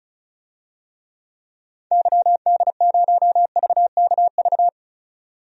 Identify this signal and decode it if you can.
Morse code, heard as a keyed tone, about 36 wpm, 700 Hz. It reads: YD0VKV
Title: DXCC-100 Call Signs - 6 Characters - 35wpm